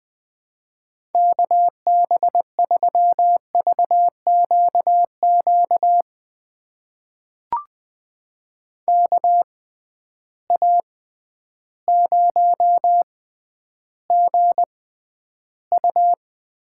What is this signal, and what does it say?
Morse code, heard as a keyed tone, about 21 wpm, spaced as for 11 wpm, 700 Hz. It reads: KB3VQQ KA0GU
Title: USA Call Signs - 6 Characters - 6x Character Spacing - 20wpm